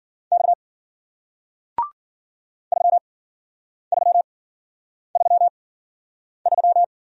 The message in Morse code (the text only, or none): X 3333